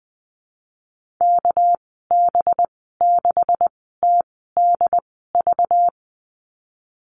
Morse code, heard as a keyed tone, about 20 wpm, spaced as for 15 wpm, 700 Hz. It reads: KB6TDV
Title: 200 Lesson 32g - Letter k - Call Signs 20wpm